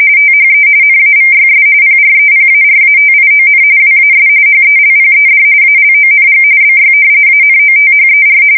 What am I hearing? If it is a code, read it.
RTTY, about 45.45 baud. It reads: THE NAVYS PREPARATION WAS HELPED ALONG BY ANOTHER
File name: fsk170 / usnavy4